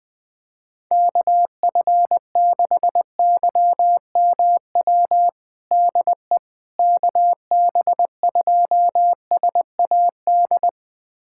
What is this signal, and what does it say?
Morse code, heard as a keyed tone, about 20 wpm, 700 Hz. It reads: KF6YMW DE KB2SAD
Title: USA Callsign DE Callsign 20wpm